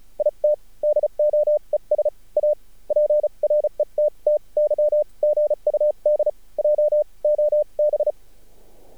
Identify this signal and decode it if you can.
Morse code, heard as a keyed tone, about 34 wpm, spaced as for 29 wpm, 600 Hz. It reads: IT DOES A PRETTY GUD JOB
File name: archived / CRSnet